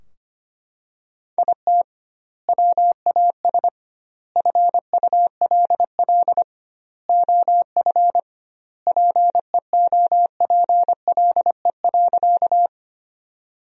Morse code, heard as a keyed tone, about 25 wpm, 700 Hz. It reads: IT WAS FULL OF PEOPLE.
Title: 2x Word-Spacing - Sentences from Top 500 Words - 25wpm